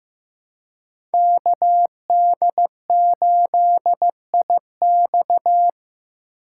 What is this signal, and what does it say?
Morse code, heard as a keyed tone, about 15 wpm, 700 Hz. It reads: KD8IX
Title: USA Call Signs - 5 Characters - 3x Character Spacing - 15wpm